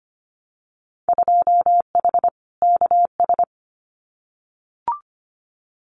Morse code, heard as a keyed tone, about 25 wpm, 700 Hz. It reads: 2H KS
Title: Winter Field Day Exchanges Encore 25wpm